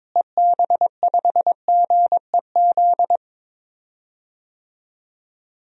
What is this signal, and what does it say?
Morse code, heard as a keyed tone, about 22 wpm, 700 Hz. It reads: EB5GEZ
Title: DXCC-100 Call Signs - 6 Characters - Repeated 2x - 22wpm